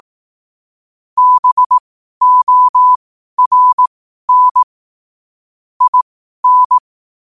Morse code, heard as a keyed tone, about 18 wpm, spaced as for 12 wpm, 1000 Hz. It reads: BORN IN
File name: qotdfiles / 12wpm